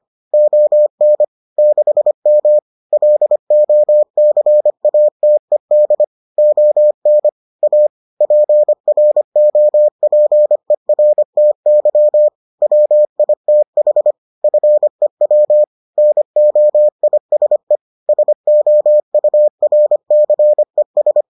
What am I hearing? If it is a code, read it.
Morse code, heard as a keyed tone, about 25 wpm, 600 Hz. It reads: ON 6M LOCATED ON A PROPERTY WITH FEW NOISE SOURCES